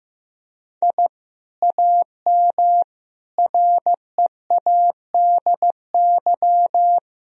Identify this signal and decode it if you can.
Morse code, heard as a keyed tone, about 15 wpm, 700 Hz. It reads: I AM READY
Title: Sentences from Top 1000 Words - Review Part 2 - 15wpm